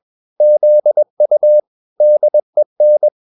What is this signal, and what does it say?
Morse code, heard as a keyed tone, about 21 wpm, 600 Hz. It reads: ZU DEN
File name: marienkind2 21wpm